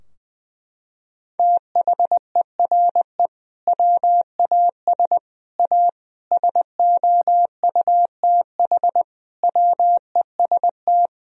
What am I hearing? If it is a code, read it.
Morse code, heard as a keyed tone, about 21 wpm, 700 Hz. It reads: THERE WAS A SOUTH WEST
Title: Sentences from Top 1000 Words - Review Part 2 - 20wpm